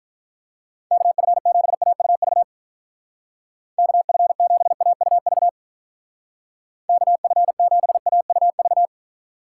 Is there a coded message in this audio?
Yes: Morse code from 0.9 to 8.9 s, heard as a keyed tone, about 46 wpm, 700 Hz. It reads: KF7AUV KF7AUV KF7AUV